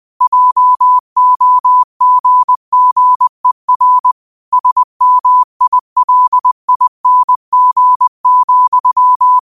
Morse code, heard as a keyed tone, about 20 wpm, 1000 Hz. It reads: JOGGER SMILING,
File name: qotdfiles / 20wpm